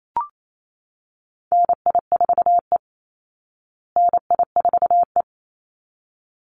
Morse code, heard as a keyed tone, about 28 wpm, 700 Hz. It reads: NI4E NI4E
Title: Call Signs Repeated 3x - 28wpm